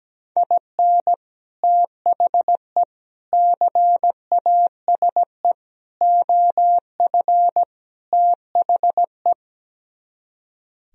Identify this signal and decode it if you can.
Morse code, heard as a keyed tone, about 17 wpm, 700 Hz. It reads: IN THE CASE OF THE
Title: Speed-Racing - Sets of 5 Words 17wpm